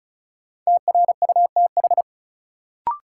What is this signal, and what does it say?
Morse code, heard as a keyed tone, about 36 wpm, 700 Hz. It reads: TRUTH
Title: Sentences from Top 800 Words - New Word Drill - 35wpm